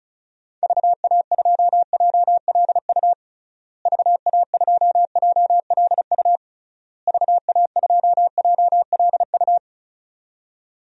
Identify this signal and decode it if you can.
Morse code, heard as a keyed tone, about 36 wpm, 700 Hz. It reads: VA2JLU VA2JLU VA2JLU
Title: DXCC-100 Call Signs - 6 Characters - Repeated 3x - 35wpm